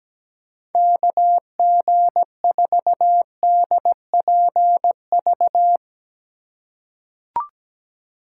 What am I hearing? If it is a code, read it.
Morse code, heard as a keyed tone, about 17 wpm, 700 Hz. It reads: KG4DPV
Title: USA Call Signs - 6 Characters - 2x Character Spacing - 17wpm